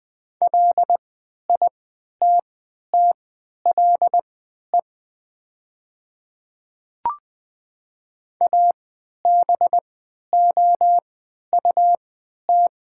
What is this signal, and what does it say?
Morse code, heard as a keyed tone, about 20 wpm, spaced as for 11 wpm, 700 Hz. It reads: LITTLE ABOUT
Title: 3x Character Spacing - Top 300 Words - 20wpm